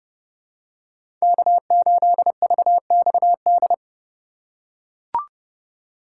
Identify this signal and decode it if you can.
Morse code, heard as a keyed tone, about 32 wpm, 700 Hz. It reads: K8VXD
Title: USA Call Signs - 5 Characters - 2x Character Spacing - 30wpm